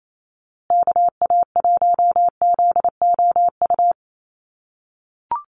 Morse code, heard as a keyed tone, about 28 wpm, 700 Hz. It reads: KA1ZOU
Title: USA Call Signs Repeated 2x 28wpm